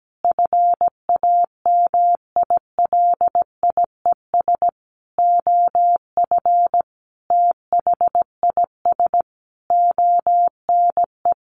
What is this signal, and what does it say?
Morse code, heard as a keyed tone, about 17 wpm, 700 Hz. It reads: FAMILIES OF THIS ONE
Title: A Sound Of Thunder 17wpm